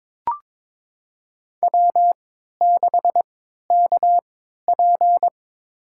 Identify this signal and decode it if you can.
Morse code, heard as a keyed tone, about 22 wpm, spaced as for 13 wpm, 700 Hz. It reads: W6KP
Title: USA Call Signs - 4 Characters - 3x Character Spacing - 22wpm